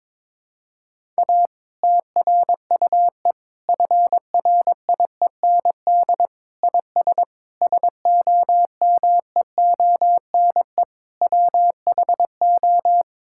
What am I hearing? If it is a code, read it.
Morse code, heard as a keyed tone, about 22 wpm, 700 Hz. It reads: A TRUE FRIEND IS SOMEONE WHO
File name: Difficult Sentences with Hints - 22wpm